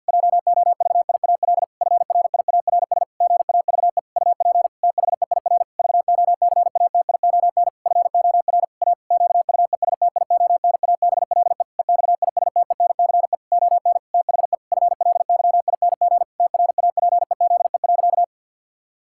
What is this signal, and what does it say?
Morse code, heard as a keyed tone, about 50 wpm, 700 Hz. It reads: JOVIAL FRIARS GAVE UP THEIR VOCATION FOR A QUESTIONABLE EXISTENCE ON THE FLYING TRAPEZE.
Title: Pangram Challenge v2 - 50wpm